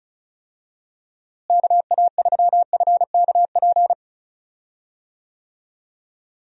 Morse code, heard as a keyed tone, about 34 wpm, 700 Hz. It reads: KA3FKP
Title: USA Call Signs - 6 Characters - Repeated 3x - 35wpm